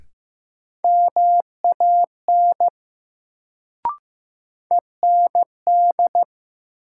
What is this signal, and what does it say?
Morse code, heard as a keyed tone, about 15 wpm, 700 Hz. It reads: MAN END
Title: Sentences from Top 200 Words - New Word Drill - 15wpm